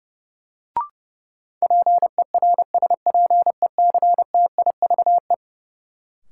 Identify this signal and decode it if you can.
Morse code, heard as a keyed tone, about 30 wpm, 700 Hz. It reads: PERSPECTIVE
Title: Sentences from Top 1200 Words - New Word Drill Alt - 30wpm